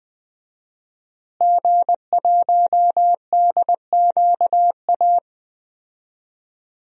Morse code, heard as a keyed tone, about 20 wpm, 700 Hz. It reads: G1DQA
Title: DXCC-100 Call Signs - 5 Characters - Repeated 2x - 20wpm